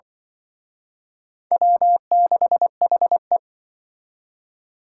Morse code, heard as a keyed tone, about 24 wpm, 700 Hz. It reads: W6HE
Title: Speed Racing - USA 1x2 Call Signs 20wpm